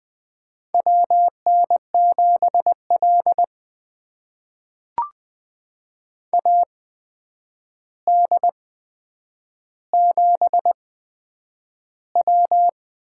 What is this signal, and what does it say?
Morse code, heard as a keyed tone, about 20 wpm, spaced as for 9 wpm, 700 Hz. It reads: WN7L AD7W
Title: USA Call Signs - 4 Characters - 8x Character Spacing - 20wpm